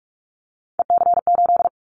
Morse code, heard as a keyed tone, about 44 wpm, 700 Hz. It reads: EC8
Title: Callsign Prefixes for Top 100 DXCC - 45wpm